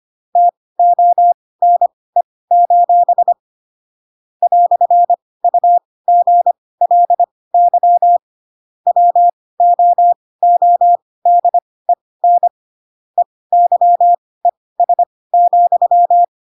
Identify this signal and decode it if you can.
Morse code, heard as a keyed tone, about 25 wpm, spaced as for 18 wpm, 700 Hz. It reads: TONE: "UGLY WOODEN EYES,
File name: Pinocchio ch3 12 25 ik1hge